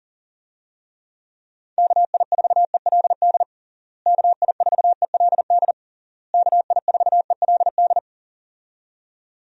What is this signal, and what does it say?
Morse code, heard as a keyed tone, about 40 wpm, 700 Hz. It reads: KI4ELD KI4ELD KI4ELD
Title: USA Call Signs - 6 Characters - Repeated 3x - 40wpm